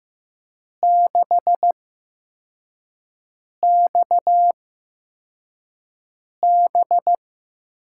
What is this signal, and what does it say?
Morse code, heard as a keyed tone, about 15 wpm, spaced as for 4 wpm, 700 Hz. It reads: 6XB
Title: USA Call Signs - 5 Characters - 8x Character Spacing - 15wpm